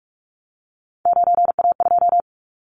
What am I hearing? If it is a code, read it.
Morse code, heard as a keyed tone, about 46 wpm, 700 Hz. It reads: 9A2
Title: Callsign Prefixes for Top 50 DXCC - 45wpm